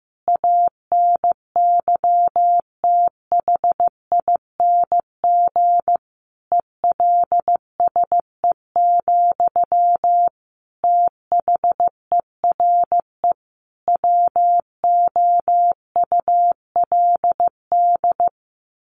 Morse code, heard as a keyed tone, about 15 wpm, 700 Hz. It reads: ANYTHING ELSE, THERE WOULD
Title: Sentences from Top 1200 Words 15wpm